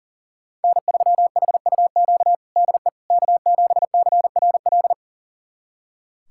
Morse code, heard as a keyed tone, about 40 wpm, 700 Hz. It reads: N3HUQ DE K7CRL
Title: USA Callsign DE Callsign - 5 Characters - 40wpm